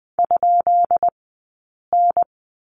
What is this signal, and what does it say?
Morse code, heard as a keyed tone, about 20 wpm, 700 Hz. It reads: ? N
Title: 144 Lesson 24d - Letter y - Two Characters 20wpm